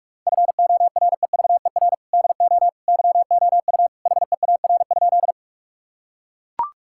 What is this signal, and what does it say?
Morse code, heard as a keyed tone, about 43 wpm, 700 Hz. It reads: FOREVER DO YOU HEAR?
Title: Sentences from Top 1600 Words 45wpm